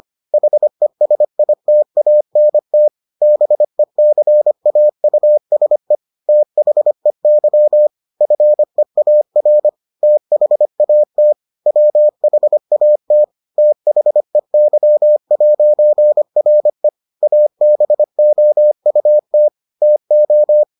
Morse code, heard as a keyed tone, about 25 wpm, 600 Hz. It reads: HESITANT BECAUSE THEY FEAR THAT WHAT THEY'RE ABOUT TO